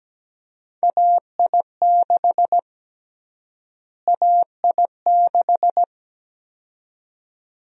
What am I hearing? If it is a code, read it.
Morse code, heard as a keyed tone, about 17 wpm, 700 Hz. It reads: AI6 AI6
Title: USA Call Sign Prefixes - Repeated 2x - 17wpm